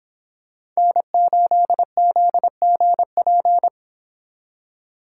Morse code, heard as a keyed tone, about 26 wpm, 700 Hz. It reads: N8ZGP